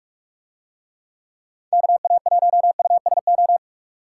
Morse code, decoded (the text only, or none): KA1USO